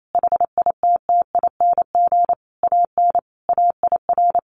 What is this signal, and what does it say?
Morse code, heard as a keyed tone, about 28 wpm, 700 Hz. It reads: HITTING AN AIR
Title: A Sound Of Thunder 28wpm